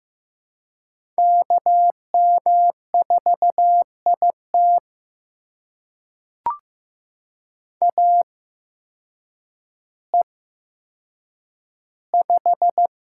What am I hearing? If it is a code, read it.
Morse code, heard as a keyed tone, about 15 wpm, spaced as for 8 wpm, 700 Hz. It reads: KM4IT AE5
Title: USA Call Signs - 5 Characters - 8x Character Spacing - 15wpm